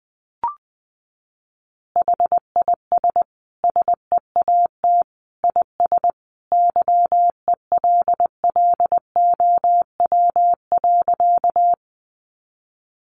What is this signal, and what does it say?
Morse code, heard as a keyed tone, about 20 wpm, 700 Hz. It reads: HIS SEAT IS YELLOW.
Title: Additional Easy Phrases 20wpm